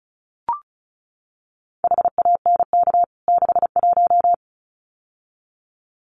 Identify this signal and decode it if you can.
Morse code, heard as a keyed tone, about 34 wpm, 700 Hz. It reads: HANK 61
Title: CWT Exchange 35wpm